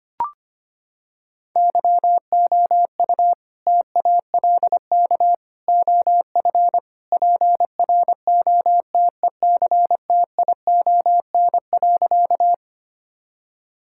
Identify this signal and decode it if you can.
Morse code, heard as a keyed tone, about 25 wpm, 700 Hz. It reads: YOU TALK OF PROTECTION.
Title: Sentences from Top 1600 Words 25wpm